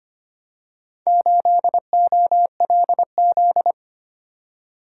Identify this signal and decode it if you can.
Morse code, heard as a keyed tone, about 25 wpm, 700 Hz. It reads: 8OLZ
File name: practice / 25wpm / Four Letter-Number Combinations - 25wpm